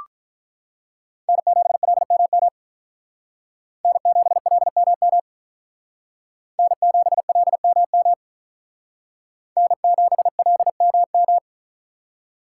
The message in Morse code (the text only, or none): N7LMM N7LMM N7LMM N7LMM